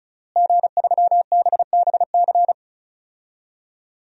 Morse code, heard as a keyed tone, about 36 wpm, 700 Hz. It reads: G3BBC
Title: DXCC-100 Call Signs - 5 Characters - Repeated 3x - 35wpm